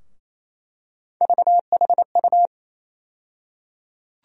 Morse code, heard as a keyed tone, about 28 wpm, 700 Hz. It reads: VHU
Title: USA Call Sign Postfixes - 3 Letters 28wpm